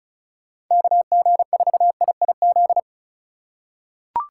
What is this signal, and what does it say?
Morse code, heard as a keyed tone, about 36 wpm, 700 Hz. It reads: KG4IIZ
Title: USA Call Signs - 6 Characters - 2x Character Spacing - 35wpm